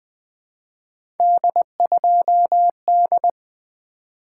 Morse code, heard as a keyed tone, about 20 wpm, 700 Hz. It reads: D2D
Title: Three Letter-Number Combinations - 20wpm